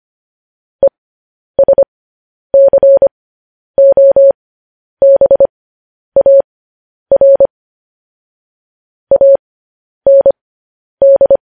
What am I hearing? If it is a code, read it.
Morse code, heard as a keyed tone, about 25 wpm, spaced as for 10 wpm, 550 Hz. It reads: ESCOBAR AND
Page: news headlines in morse code at 10 wpm